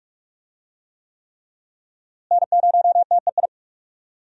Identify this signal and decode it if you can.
Morse code, heard as a keyed tone, about 41 wpm, 700 Hz. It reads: N0TEI
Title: USA Call Signs - 5 Characters - Repeated 3x - 45wpm